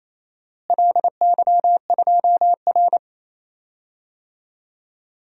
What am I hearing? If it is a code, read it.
Morse code, heard as a keyed tone, about 27 wpm, 700 Hz. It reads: LY2R
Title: DXCC-100 Call Signs - 4 Characters - Repeated 2x - 28wpm